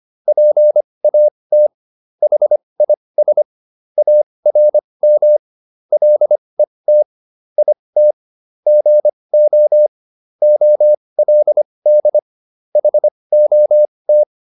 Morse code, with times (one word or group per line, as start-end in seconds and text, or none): 0.3-1.7 PAT
2.2-3.4 HIS
4.0-5.4 ARM
5.9-7.0 LET
7.6-8.1 IT
8.7-9.9 GO
10.4-12.2 OLD
12.8-14.2 HOT